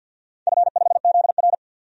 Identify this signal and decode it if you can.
Morse code, heard as a keyed tone, about 49 wpm, 700 Hz. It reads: F5ZR
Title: DXCC-100 Call Signs - 4 Characters - 50wpm